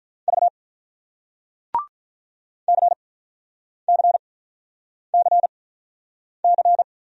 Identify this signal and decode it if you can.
Morse code, heard as a keyed tone, about 48 wpm, spaced as for 9 wpm, 690 Hz. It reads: V CCCC